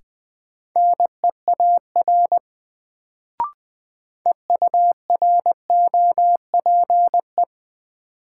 Morse code, heard as a keyed tone, about 20 wpm, 700 Hz. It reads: NEAR EUROPE
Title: Sets of 5 Words Encore - New Word Drill - 20wpm